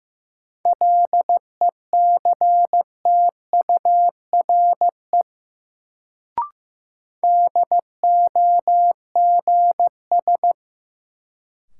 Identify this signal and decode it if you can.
Morse code, heard as a keyed tone, about 16 wpm, 700 Hz. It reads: LECTURE DOGS
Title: Sentences from Top 1600 Words - New Word Drill Alt - 15wpm